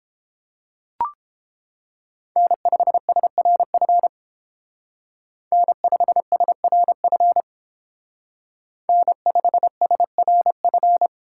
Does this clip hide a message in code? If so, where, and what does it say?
Yes: Morse code from 2.4 to 11.1 s, heard as a keyed tone, about 29 wpm, 700 Hz. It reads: N5SRF N5SRF N5SRF